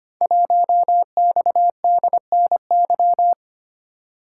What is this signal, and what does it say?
Morse code, heard as a keyed tone, about 25 wpm, 700 Hz. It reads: 1XDNY
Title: Code Groups - 5 Letter:Numbers - 25wpm